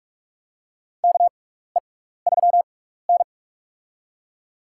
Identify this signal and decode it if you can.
Morse code, heard as a keyed tone, about 43 wpm, spaced as for 15 wpm, 700 Hz. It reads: KE3N
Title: USA Call Signs - 4 Characters - 6x Character Spacing - 45wpm